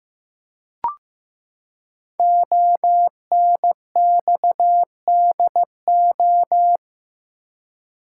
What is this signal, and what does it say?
Morse code, heard as a keyed tone, about 15 wpm, 700 Hz. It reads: ONXDO
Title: Code Groups - 5 Letters - 15wpm